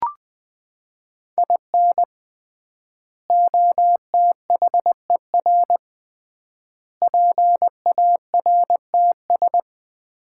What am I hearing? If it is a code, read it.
Morse code, heard as a keyed tone, about 20 wpm, 700 Hz. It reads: IN OTHER PARTS
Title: 3x Word-Spacing - Sets of 5 Words - 20wpm